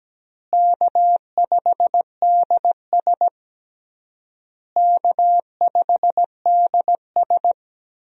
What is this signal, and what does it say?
Morse code, heard as a keyed tone, about 17 wpm, 700 Hz. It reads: K5DS K5DS
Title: USA Call Signs - 4 Characters - Repeated 3x - 17wpm